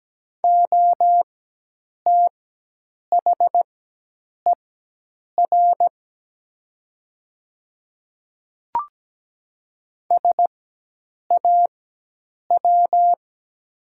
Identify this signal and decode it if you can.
Morse code, heard as a keyed tone, about 17 wpm, spaced as for 8 wpm, 700 Hz. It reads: OTHER SAW
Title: 4x Character Spacing - Top 300 Words - 17wpm